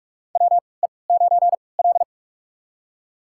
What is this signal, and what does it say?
Morse code, heard as a keyed tone, about 46 wpm, spaced as for 25 wpm, 700 Hz. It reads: WE9L
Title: USA Call Signs - 3x Character Spacing - 45wpm